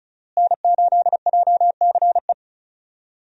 Morse code, heard as a keyed tone, about 36 wpm, 700 Hz. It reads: N8JCE
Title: USA Call Signs - 5 Characters - Repeated 3x - 35wpm